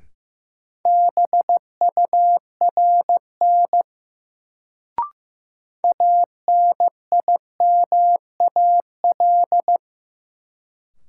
Morse code, heard as a keyed tone, about 15 wpm, 700 Hz. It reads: BURN ANIMAL
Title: Sentences from Top 1300 Words - New Word Drill Alt - 15wpm